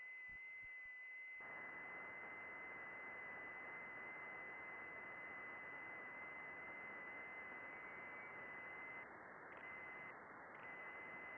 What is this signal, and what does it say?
FT8.
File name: contests / june-vhf-k5tr- -0200